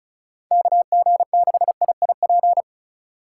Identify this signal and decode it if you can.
Morse code, heard as a keyed tone, about 34 wpm, 700 Hz. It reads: KG6IIP
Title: K1USN SST QSO Practice - 35wpm